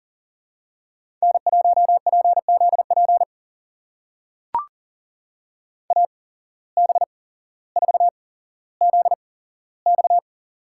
Morse code, heard as a keyed tone, about 42 wpm, spaced as for 17 wpm, 700 Hz. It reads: N1PZP AB4ZX
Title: USA Call Signs - 5 Characters - 8x Character Spacing - 40wpm